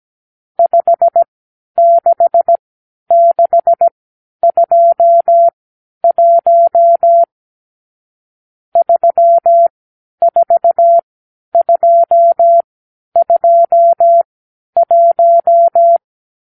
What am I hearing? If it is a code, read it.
Morse code, heard as a keyed tone, about 17 wpm, spaced as for 10 wpm, 680 Hz. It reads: 56621 34221
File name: k7qo / Lesson 52 (8)